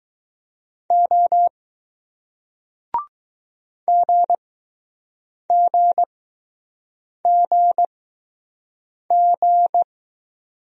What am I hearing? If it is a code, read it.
Morse code, heard as a keyed tone, about 20 wpm, spaced as for 6 wpm, 700 Hz. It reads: O GGGG